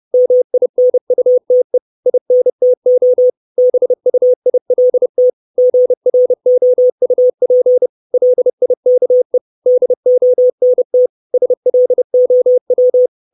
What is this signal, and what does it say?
Morse code, heard as a keyed tone, about 30 wpm, 500 Hz. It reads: MINUTE INTO BUILT GROUP LIKE DONT SLOW
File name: 632 Words 30WPM